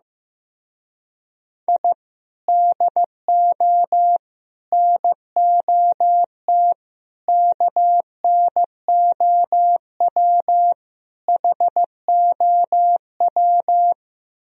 Morse code, heard as a keyed tone, about 15 wpm, 700 Hz. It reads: I DO NOT KNOW HOW
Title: Speed-Racing - Sets of 5 Words 15wpm